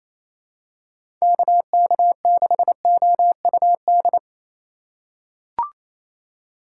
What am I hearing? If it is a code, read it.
Morse code, heard as a keyed tone, about 28 wpm, 700 Hz. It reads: KK6OUD
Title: USA Call Signs - 6 Characters - 8x Character Spacing - 28wpm